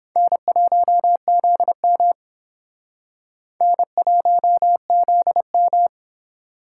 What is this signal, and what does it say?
Morse code, heard as a keyed tone, about 30 wpm, 700 Hz. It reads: N1ZM N1ZM